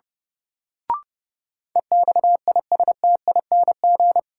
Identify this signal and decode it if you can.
Morse code, heard as a keyed tone, about 30 wpm, 700 Hz. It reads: EXISTING
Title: Sentences from Top 1500 Words - New Word Drill - 30wpm